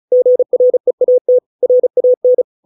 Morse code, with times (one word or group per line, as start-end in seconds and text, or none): 0.1-1.4 GREAT
1.6-2.4 RAN